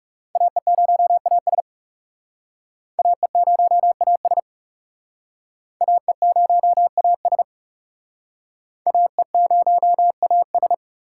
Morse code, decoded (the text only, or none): AE0AS AE0AS AE0AS AE0AS